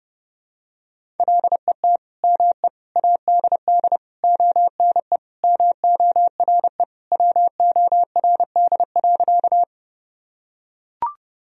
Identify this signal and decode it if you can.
Morse code, heard as a keyed tone, about 32 wpm, 700 Hz. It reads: LET ME ADD ONE MORE WORD.
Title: Sentences from Top 1000 Words - Review Part 1 - 30wpm